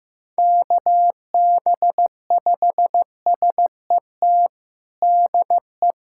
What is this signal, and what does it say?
Morse code, heard as a keyed tone, about 16 wpm, 700 Hz. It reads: KB5SET DE
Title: USA Callsign DE Callsign - 6 Characters - 15wpm